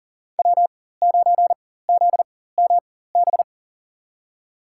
Morse code, heard as a keyed tone, about 40 wpm, spaced as for 19 wpm, 700 Hz. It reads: W9ZMB
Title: USA Call Signs - 4x Character Spacing - 40wpm